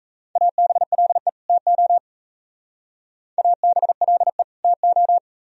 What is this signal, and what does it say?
Morse code, heard as a keyed tone, about 35 wpm, 700 Hz. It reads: ABLE TO ABLE TO